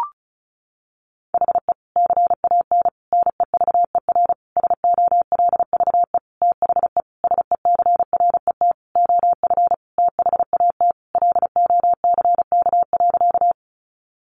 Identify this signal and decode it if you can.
Morse code, heard as a keyed tone, about 34 wpm, 700 Hz. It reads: HE CAN NEVER SOLVE THE SECRET OF THAT LOCK.
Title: Sentences from Top 1300 Words 35wpm